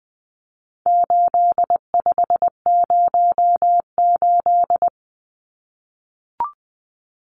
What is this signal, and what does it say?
Morse code, heard as a keyed tone, about 20 wpm, 700 Hz. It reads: 8508
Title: Numbers - 4-Digit 20wpm